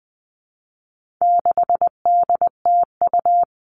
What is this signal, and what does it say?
Morse code, heard as a keyed tone, about 20 wpm, 700 Hz. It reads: 6DTU
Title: All Cut Numbers - 4-Digit 20wpm